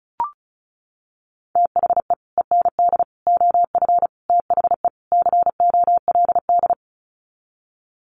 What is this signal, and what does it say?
Morse code, heard as a keyed tone, about 34 wpm, 700 Hz. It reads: THE END OF THE COLD
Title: Sets of 5 Words 35wpm